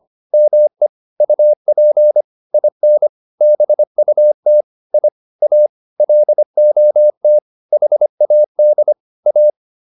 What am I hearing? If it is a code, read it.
Morse code, heard as a keyed tone, about 25 wpm, 600 Hz. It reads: ME UP IN BUT I A LOT HAD A